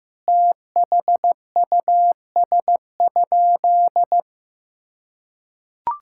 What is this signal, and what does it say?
Morse code, heard as a keyed tone, about 15 wpm, 700 Hz. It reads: THUS?
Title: Sentences from Top 1000 Words - Review Part 5 - 15wpm